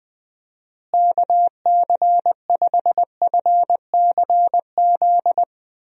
Morse code, heard as a keyed tone, about 21 wpm, 700 Hz. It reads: KC5FCZ
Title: USA Call Signs - 6 Characters - Repeated 2x - 20wpm